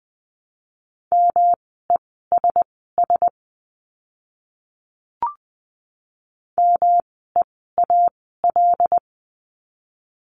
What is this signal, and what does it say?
Morse code, heard as a keyed tone, about 20 wpm, spaced as for 14 wpm, 700 Hz. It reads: MESS MEAL
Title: 108 Lesson 18f - Letter m - Words 20wpm